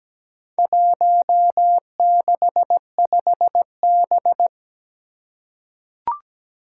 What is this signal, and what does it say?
Morse code, heard as a keyed tone, about 17 wpm, 700 Hz. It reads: 165B
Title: Four Challenging Letter-Number Combinations 17wpm